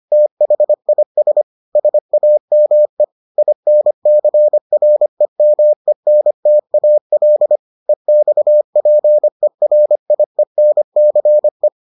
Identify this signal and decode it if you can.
Morse code, heard as a keyed tone, about 25 wpm, 600 Hz. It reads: THIS SAME INCREMENTAL EXPERIENCE